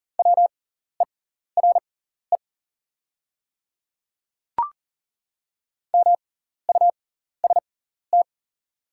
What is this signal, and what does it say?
Morse code, heard as a keyed tone, about 41 wpm, spaced as for 14 wpm, 690 Hz. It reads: WERE MUST
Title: 6x Character Spacing - Top 300 Words - 40wpm